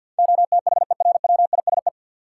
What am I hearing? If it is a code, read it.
Morse code, heard as a keyed tone, about 48 wpm, 700 Hz. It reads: OTHERWISE
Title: Sentences from Top 1200 Words - New Word Drill Alt - 50wpm